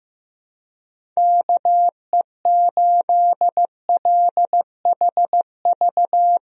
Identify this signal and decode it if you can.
Morse code, heard as a keyed tone, about 15 wpm, 700 Hz. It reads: KE8LHV
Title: USA Call Signs - 6 Characters - Repeated 3x - 15wpm